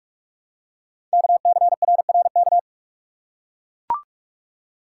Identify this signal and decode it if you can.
Morse code, heard as a keyed tone, about 44 wpm, 700 Hz. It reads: KCRRK
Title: Code Groups - 5 Letter:Numbers - 45wpm